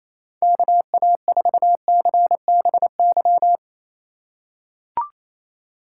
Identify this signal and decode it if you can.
Morse code, heard as a keyed tone, about 27 wpm, 700 Hz. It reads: KA4CBY
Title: USA Call Signs - 6 Characters - 6x Character Spacing - 28wpm